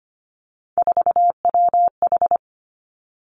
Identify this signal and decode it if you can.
Morse code, heard as a keyed tone, about 25 wpm, 700 Hz. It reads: 4WH
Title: Three Challenging Letter-Number Combinations 25wpm